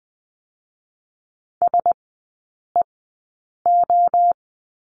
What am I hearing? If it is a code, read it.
Morse code, heard as a keyed tone, about 20 wpm, 700 Hz. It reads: S E O
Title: 035 Lesson 7e - Letter s - Three Characters 20wpm